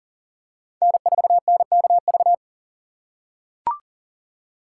Morse code, heard as a keyed tone, about 40 wpm, 700 Hz. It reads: N4NKV
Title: USA Call Signs - 5 Characters - 6x Character Spacing - 40wpm